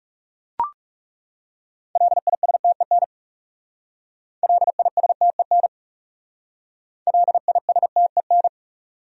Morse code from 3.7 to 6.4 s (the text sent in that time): LISTEN